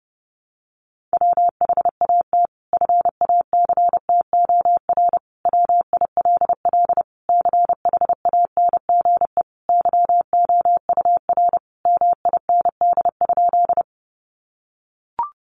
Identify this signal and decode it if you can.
Morse code, heard as a keyed tone, about 30 wpm, 700 Hz. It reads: WHAT FACTOR WILL CHANGE YOUR MIND?
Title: Sentences from Top 1300 Words 30wpm